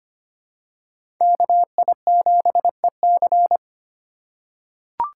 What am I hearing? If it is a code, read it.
Morse code, heard as a keyed tone, about 25 wpm, 700 Hz. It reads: KI7EC
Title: USA Call Signs - 5 Characters - 3x Character Spacing - 25wpm